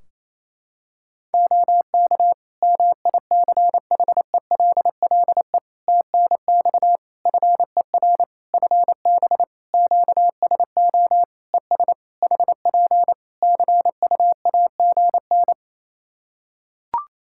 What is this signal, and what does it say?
Morse code, heard as a keyed tone, about 28 wpm, 700 Hz. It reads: OK MICHELLE TNX FER FB QSO ES HP CUAGN
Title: Practice QSO - Exchange 4 - Ending - 28wpm